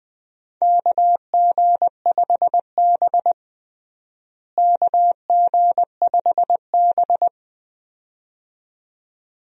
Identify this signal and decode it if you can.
Morse code, heard as a keyed tone, about 21 wpm, 700 Hz. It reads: KG5B KG5B
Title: USA Call Signs - 4 Characters - Repeated 2x - 20wpm